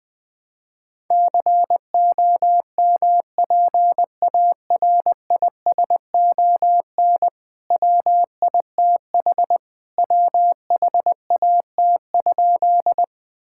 Morse code, heard as a keyed tone, about 20 wpm, 700 Hz. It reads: COMPARISON WITH WHAT?
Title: Sentences from Top 1600 Words 20wpm